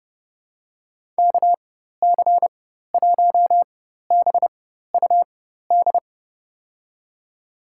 Morse code, heard as a keyed tone, about 30 wpm, spaced as for 14 wpm, 700 Hz. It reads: KC1BUD
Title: USA Call Signs - 6 Characters - 4x Character Spacing - 30wpm